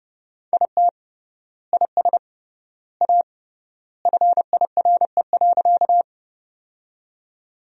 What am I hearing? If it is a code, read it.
Morse code, heard as a keyed tone, about 29 wpm, 700 Hz. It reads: IT IS A FIRE.
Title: 3x Word-Spacing - Sentences from Top 400 Words - 30wpm